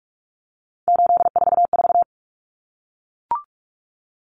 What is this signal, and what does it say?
Morse code, heard as a keyed tone, about 44 wpm, 700 Hz. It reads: 844
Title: Common Cut Numbers - 3-Digit 45wpm